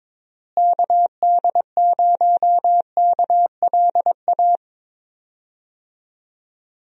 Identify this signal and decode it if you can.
Morse code, heard as a keyed tone, about 22 wpm, 700 Hz. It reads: KD0KLA
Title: USA Call Signs - 6 Characters - Repeated 3x - 22wpm